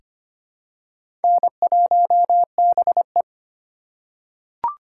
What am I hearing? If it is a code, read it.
Morse code, heard as a keyed tone, about 25 wpm, 700 Hz. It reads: N1BE
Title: USA Call Signs - 2x Character Spacing - 25wpm